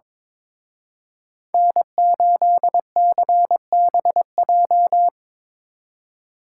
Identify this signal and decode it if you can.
Morse code, heard as a keyed tone, about 26 wpm, 700 Hz. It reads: N8CBJ
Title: Speed Racing - USA 1x3 Call Signs 22wpm